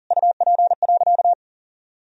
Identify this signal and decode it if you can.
Morse code, heard as a keyed tone, about 37 wpm, 700 Hz. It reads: UP.
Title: Sentences from Top 1000 Words - Review Part 1 - 40wpm